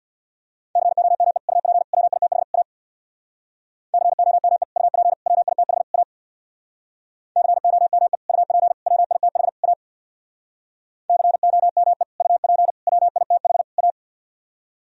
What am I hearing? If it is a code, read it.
Morse code, heard as a keyed tone, about 64 wpm, 700 Hz. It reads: COME UP WITH A COME UP WITH A COME UP WITH A COME UP WITH A